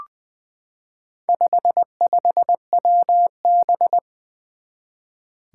Morse code, heard as a keyed tone, about 20 wpm, 700 Hz. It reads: 55WB
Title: Four Challenging Letter-Number Combinations 20wpm